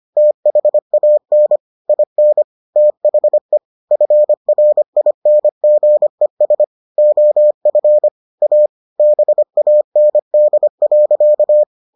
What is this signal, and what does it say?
Morse code, heard as a keyed tone, about 25 wpm, 600 Hz. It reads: THAN IN THE FRINGES OF A BAND.